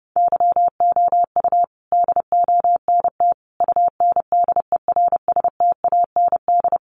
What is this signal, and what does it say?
Morse code, heard as a keyed tone, about 28 wpm, 700 Hz. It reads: YOU DONT UNDERSTAND
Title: Sentences from Top 1000 Words 30wpm